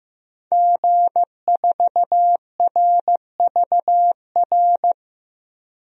Morse code, heard as a keyed tone, about 15 wpm, 700 Hz. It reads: G4RVR
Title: DXCC-100 Call Signs - 5 Characters - 15wpm